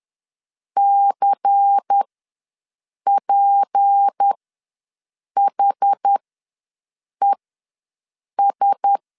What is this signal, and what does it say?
Morse code, heard as a keyed tone, about 10 wpm, spaced as for 6 wpm, 800 Hz. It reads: CPHES
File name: LEZIONI / 6 PKX0 / 6 P ws 10wpm 3751